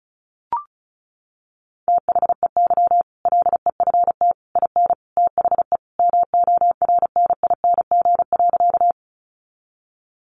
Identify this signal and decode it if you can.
Morse code, heard as a keyed tone, about 34 wpm, 700 Hz. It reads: THEY LEFT IN THE MORNING.
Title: Sentences from Top 400 Words 35wpm